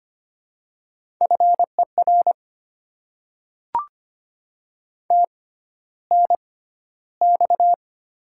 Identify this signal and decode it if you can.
Morse code, heard as a keyed tone, about 25 wpm, spaced as for 13 wpm, 700 Hz. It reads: FER TNX
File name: Top 20 QSO Elements - 6x Character Spacing - 25wpm